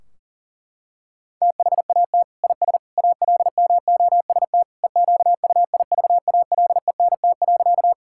Morse code, heard as a keyed tone, about 37 wpm, 700 Hz. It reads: THAT IS ALMOST EQUIVALENT.
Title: Sentences from Top 1600 Words 40wpm